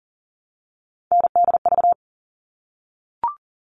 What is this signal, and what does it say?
Morse code, heard as a keyed tone, about 40 wpm, 700 Hz. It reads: NDV
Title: All Cut Numbers - 3-Digit 40wpm